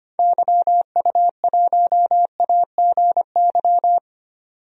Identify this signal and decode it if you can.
Morse code, heard as a keyed tone, about 25 wpm, 700 Hz. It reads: YU1AGY
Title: DXCC-100 Call Signs - 6 Characters - 25wpm